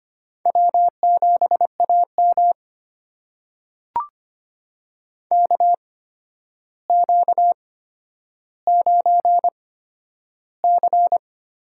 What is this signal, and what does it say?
Morse code, heard as a keyed tone, about 25 wpm, spaced as for 11 wpm, 700 Hz. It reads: W7AM KQ9C
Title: USA Call Signs - 4 Characters - 8x Character Spacing - 25wpm